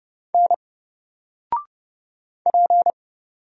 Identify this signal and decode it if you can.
Morse code, heard as a keyed tone, about 30 wpm, 700 Hz. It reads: N P
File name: Single-Letter-Number-30wpm